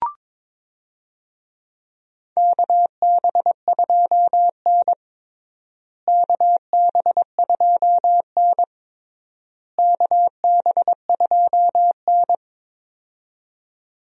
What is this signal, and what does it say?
Morse code, heard as a keyed tone, about 22 wpm, 700 Hz. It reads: KB2N KB2N KB2N
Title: USA Call Signs - 4 Characters - Repeated 3x - 22wpm